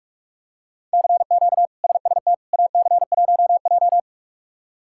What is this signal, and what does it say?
Morse code, heard as a keyed tone, about 46 wpm, 700 Hz. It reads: CQ SST AC1J
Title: K1USN SST QSO Practice - 45wpm